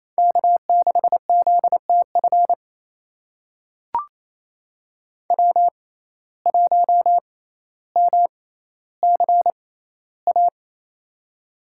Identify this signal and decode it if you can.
Morse code, heard as a keyed tone, about 28 wpm, spaced as for 14 wpm, 700 Hz. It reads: K6ZTF W1MCA
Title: USA Call Signs - 5 Characters - 6x Character Spacing - 28wpm